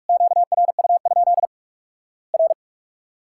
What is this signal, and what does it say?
Morse code, heard as a keyed tone, about 48 wpm, 700 Hz. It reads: QRU? R
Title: Worked All Europe DX Contest - QTC Exchange - 45wpm